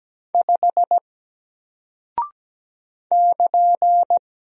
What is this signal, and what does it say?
Morse code, heard as a keyed tone, about 17 wpm, 700 Hz. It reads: 5 <KN>
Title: Single Character (Letter:Number:Punctuation:Prosign) 17wpm